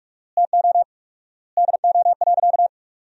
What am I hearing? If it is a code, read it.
Morse code, heard as a keyed tone, about 42 wpm, 700 Hz. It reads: TO DO.
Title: 4x Word-Spacing - Sentences from Top 200 Words - 45wpm